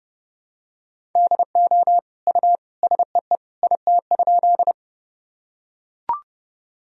Morse code, heard as a keyed tone, about 30 wpm, 700 Hz. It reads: DO U SEE IT?
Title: Sentences from Top 100 Words and Abbreviations 30wpm